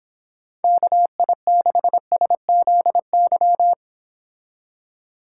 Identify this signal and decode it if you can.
Morse code, heard as a keyed tone, about 26 wpm, 700 Hz. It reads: KI6SZY